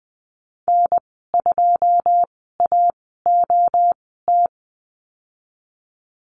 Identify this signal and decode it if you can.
Morse code, heard as a keyed tone, about 20 wpm, spaced as for 14 wpm, 700 Hz. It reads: N2AOT
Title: 082 Lesson 14g - Number 2 - Call Signs 20wpm